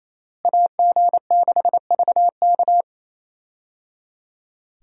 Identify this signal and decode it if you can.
Morse code, heard as a keyed tone, about 27 wpm, 700 Hz. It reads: AG6VK
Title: USA Call Signs - 5 Characters - Repeated 3x - 28wpm